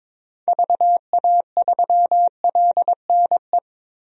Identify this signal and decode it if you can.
Morse code, heard as a keyed tone, about 22 wpm, 700 Hz. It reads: VA3LNE
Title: DXCC-100 Call Signs - 6 Characters - 22wpm